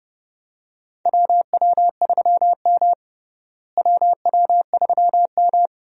Morse code, heard as a keyed tone, about 30 wpm, 700 Hz. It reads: WW3M WW3M
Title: USA Call Signs - 4 Characters - Repeated 2x - 30wpm